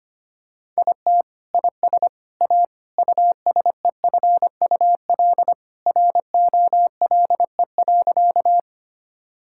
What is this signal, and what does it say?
Morse code, heard as a keyed tone, about 25 wpm, 700 Hz. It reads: IT IS A USEFUL ROLE.
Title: Sentences from Top 1000 Words - Review Part 6 - 25wpm